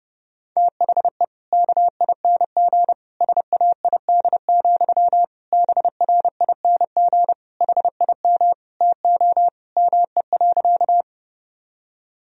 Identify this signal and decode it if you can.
Morse code, heard as a keyed tone, about 30 wpm, 700 Hz. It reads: THE KING SAID, BRING HIM TO ME.
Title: Sentences from Top 1000 Words - Review Part 3 - 30wpm